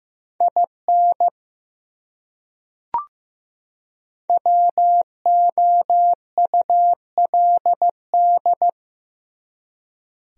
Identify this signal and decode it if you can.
Morse code, heard as a keyed tone, about 15 wpm, 700 Hz. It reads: IN WOULD
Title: Top 100 Words 15wpm